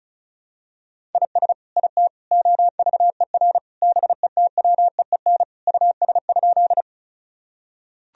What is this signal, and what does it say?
Morse code, heard as a keyed tone, about 36 wpm, 700 Hz. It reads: IS IT OVER BETWEEN US?
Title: Sentences from Top 1000 Words - Review Part 1 - 35wpm